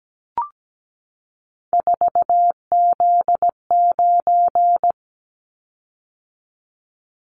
Morse code, heard as a keyed tone, about 17 wpm, 700 Hz. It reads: 4Z9
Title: Callsign Prefixes for Top 50 DXCC - 17wpm